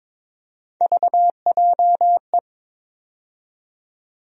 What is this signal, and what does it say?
Morse code, heard as a keyed tone, about 22 wpm, 700 Hz. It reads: VJE
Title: USA Call Sign Postfixes - 3 Letters 22wpm